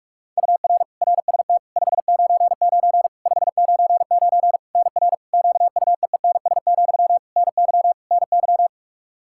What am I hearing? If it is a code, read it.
Morse code, heard as a keyed tone, about 46 wpm, 700 Hz. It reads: UR RST 599 599 NR QUEENS, NY NY
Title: K1USN SST QSO Practice 45wpm